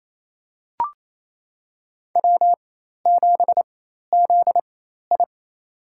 Morse code, heard as a keyed tone, about 28 wpm, spaced as for 13 wpm, 700 Hz. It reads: W7ZI
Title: USA Call Signs - 4 Characters - 4x Character Spacing - 28wpm